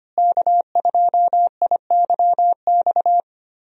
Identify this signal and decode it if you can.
Morse code, heard as a keyed tone, about 25 wpm, 700 Hz. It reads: K2IYX
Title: USA Call Signs - 5 Characters - Repeated 2x - 25wpm